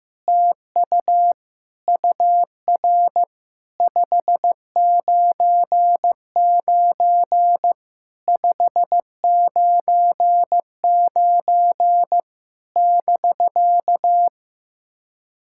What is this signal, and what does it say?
Morse code, heard as a keyed tone, about 15 wpm, 700 Hz. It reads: TU UR 599 599 <BK>
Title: SOTA QSO Practice 15wpm